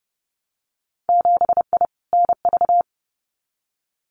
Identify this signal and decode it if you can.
Morse code, heard as a keyed tone, about 30 wpm, 700 Hz. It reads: 7I NV
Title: Winter Field Day Exchanges Encore 30wpm